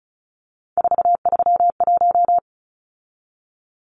Morse code, heard as a keyed tone, about 34 wpm, 700 Hz. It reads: 431
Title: Numbers - 3-Digit 35wpm